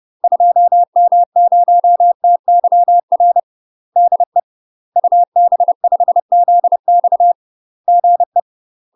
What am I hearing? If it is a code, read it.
Morse code, heard as a keyed tone, about 30 wpm, 700 Hz. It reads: 2M0TYR DE UB5ZX GE
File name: UK MASTER QSOs TRAINING / 30WPM / UK MASTER QSOs 0002